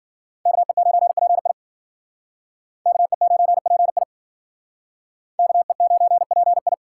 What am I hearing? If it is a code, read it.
Morse code, heard as a keyed tone, about 51 wpm, 700 Hz. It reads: KE9PI KE9PI KE9PI